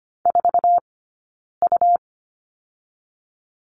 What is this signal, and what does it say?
Morse code, heard as a keyed tone, about 25 wpm, 700 Hz. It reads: 4 U
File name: Lesson 17d - Letter u - Two Characters 25wpm